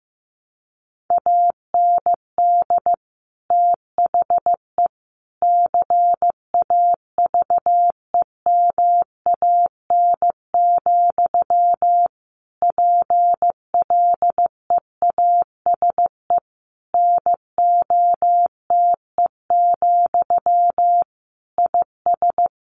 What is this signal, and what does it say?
Morse code, heard as a keyed tone, about 14 wpm, 700 Hz. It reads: AND THE CAVEMAN, PLEASE NOTE, IS
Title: A Sound Of Thunder 15wpm